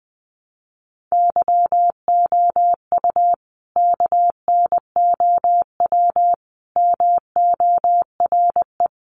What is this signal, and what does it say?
Morse code, heard as a keyed tone, about 20 wpm, 700 Hz. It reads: YOU KNOW MORE
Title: Sentences from Top 100 Words 20wpm